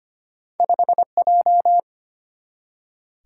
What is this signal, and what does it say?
Morse code, heard as a keyed tone, about 25 wpm, 700 Hz. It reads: 5J
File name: Two Letter-Number Combinations - 25wpm